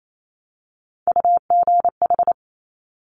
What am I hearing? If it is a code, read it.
Morse code, heard as a keyed tone, about 28 wpm, 700 Hz. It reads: UGH
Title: Top Three Letter Combinations in Words 28wpm